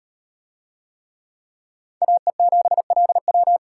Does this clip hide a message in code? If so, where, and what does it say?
Yes: Morse code from 2.0 to 3.6 s, heard as a keyed tone, about 38 wpm, 700 Hz. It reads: AE7LW